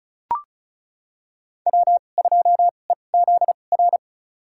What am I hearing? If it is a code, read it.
Morse code, heard as a keyed tone, about 34 wpm, spaced as for 25 wpm, 700 Hz. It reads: W2EZR
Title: USA Call Signs - 5 Characters - 2x Character Spacing - 35wpm